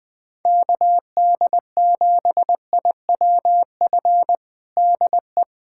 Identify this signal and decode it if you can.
Morse code, heard as a keyed tone, about 20 wpm, 700 Hz. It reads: KD7IWF DE
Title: USA Callsign DE Callsign - 6 Characters - 20wpm